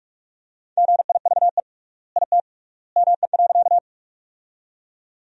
Morse code, heard as a keyed tone, about 46 wpm, 700 Hz. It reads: GIVE IT ME.
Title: 3x Word-Spacing - Sentences from Top 200 Words - 45wpm